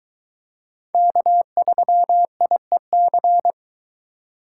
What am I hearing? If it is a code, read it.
Morse code, heard as a keyed tone, about 22 wpm, 700 Hz. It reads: K3IEC